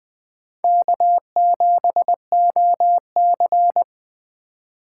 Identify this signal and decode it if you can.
Morse code, heard as a keyed tone, about 20 wpm, 700 Hz. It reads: K7OC
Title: USA Call Signs - 4 Characters - 3x Character Spacing - 20wpm